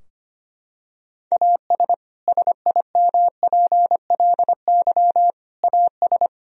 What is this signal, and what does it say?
Morse code, heard as a keyed tone, about 25 wpm, 700 Hz. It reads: AS SIMPLY AS
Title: Sentences from Top 1000 Words - Review Part 4 - 25wpm